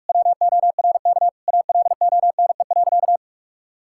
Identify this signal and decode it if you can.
Morse code, heard as a keyed tone, about 46 wpm, 700 Hz. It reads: WORK ALONE.
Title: Sentences from Top 1600 Words 45wpm